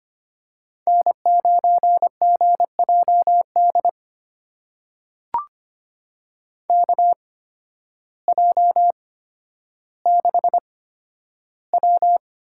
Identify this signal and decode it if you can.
Morse code, heard as a keyed tone, about 25 wpm, spaced as for 12 wpm, 700 Hz. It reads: N9GJD KJ6W
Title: USA Call Signs - 8x Character Spacing - 25wpm